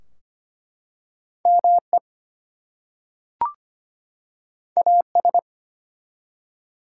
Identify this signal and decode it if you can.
Morse code, heard as a keyed tone, about 25 wpm, 700 Hz. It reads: ME AS
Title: Two Letter Words - 25wpm